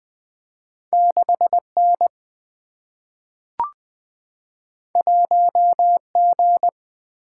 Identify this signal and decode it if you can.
Morse code, heard as a keyed tone, about 20 wpm, 700 Hz. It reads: 6N 1G
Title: Two Letter-Number Combinations - 20wpm